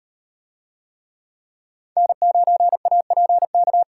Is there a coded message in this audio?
Yes: Morse code from 2.0 to 3.8 s, heard as a keyed tone, about 38 wpm, 700 Hz. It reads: N9APK